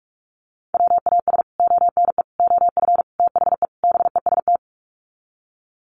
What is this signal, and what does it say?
Morse code, heard as a keyed tone, about 44 wpm, 700 Hz. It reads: WAS ONE OF THE BEST
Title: Sets of 5 Words Encore2 45wpm